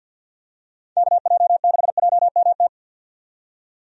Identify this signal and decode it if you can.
Morse code, heard as a keyed tone, about 52 wpm, 700 Hz. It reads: KJ6JMT
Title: USA Call Signs - 6 Characters - 50wpm